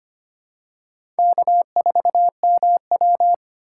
Morse code, heard as a keyed tone, about 25 wpm, 700 Hz. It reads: K4MW
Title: USA Call Signs - 4 Characters - 3x Character Spacing - 25wpm